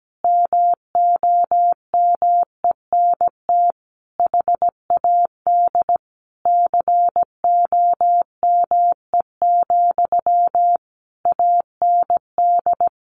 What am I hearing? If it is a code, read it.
Morse code, heard as a keyed tone, about 17 wpm, 700 Hz. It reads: MOMENT HAD COME, AND
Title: Sentences from Top 1500 Words 17wpm